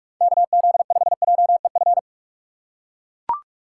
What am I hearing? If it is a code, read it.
Morse code, heard as a keyed tone, about 41 wpm, 700 Hz. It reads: KZ5JEF
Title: USA Call Signs - 6 Characters - 4x Character Spacing - 45wpm